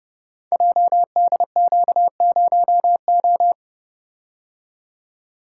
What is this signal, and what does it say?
Morse code, heard as a keyed tone, about 29 wpm, 700 Hz. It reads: JDQ0O
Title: Code Groups - 5 Letter:Numbers - 30wpm